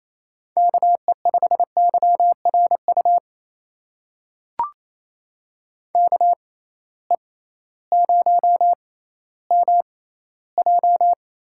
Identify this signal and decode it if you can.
Morse code, heard as a keyed tone, about 28 wpm, spaced as for 15 wpm, 700 Hz. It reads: KE5YRU KE0MJ
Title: USA Call Signs - 6 Characters - 6x Character Spacing - 28wpm